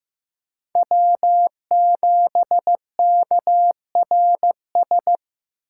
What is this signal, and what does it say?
Morse code, heard as a keyed tone, about 15 wpm, 700 Hz. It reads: W7KRS
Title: K1USN SST QSO Practice - 15wpm